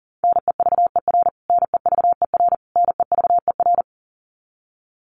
Morse code, heard as a keyed tone, about 40 wpm, 700 Hz. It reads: NEVER NEVER NEVER
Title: Top 300 Words Repeated 40wpm